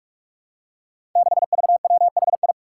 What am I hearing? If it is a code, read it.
Morse code, heard as a keyed tone, about 44 wpm, 700 Hz. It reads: 6VWHI
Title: Five Challenging Letter-Number Combinations 45wpm